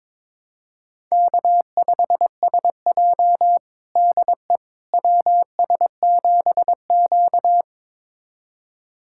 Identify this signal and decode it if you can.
Morse code, heard as a keyed tone, about 21 wpm, 700 Hz. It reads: K5SJ DE WS7Q
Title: USA Callsign DE Callsign - 4 Characters - 22wpm